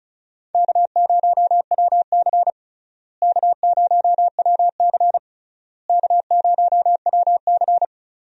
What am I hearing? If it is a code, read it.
Morse code, heard as a keyed tone, about 36 wpm, 700 Hz. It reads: K0WC K0WC K0WC
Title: USA Call Signs - 4 Characters - Repeated 3x - 35wpm